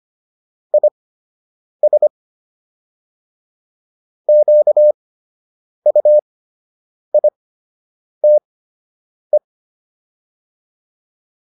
Morse code, heard as a keyed tone, about 25 wpm, spaced as for 8 wpm, 600 Hz. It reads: IS QUITE